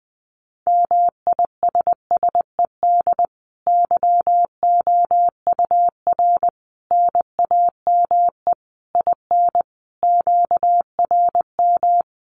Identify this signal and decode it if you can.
Morse code, heard as a keyed tone, about 20 wpm, 700 Hz. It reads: MISSED YOUR NAME IN QRM
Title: Common QSO Phrases 20wpm